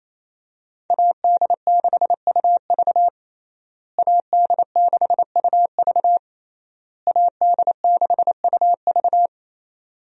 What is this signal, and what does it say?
Morse code, heard as a keyed tone, about 28 wpm, 700 Hz. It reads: AD6UV AD6UV AD6UV
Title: USA Call Signs - 5 Characters - Repeated 3x - 28wpm